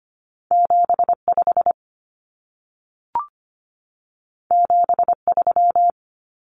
Morse code, heard as a keyed tone, about 25 wpm, 700 Hz. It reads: 75 73
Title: Numbers - 2-Digit 25wpm